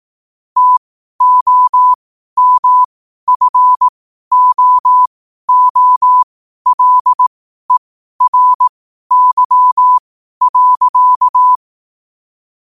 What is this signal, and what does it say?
Morse code, heard as a keyed tone, about 18 wpm, spaced as for 12 wpm, 1000 Hz. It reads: TOMFOOLERY.
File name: qotdfiles / 12wpm